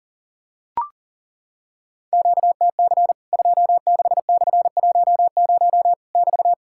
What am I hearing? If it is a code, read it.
Morse code, heard as a keyed tone, about 40 wpm, 700 Hz. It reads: QTC 26/10 <BT>
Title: Worked All Europe DX Contest - QTC Exchange - 40wpm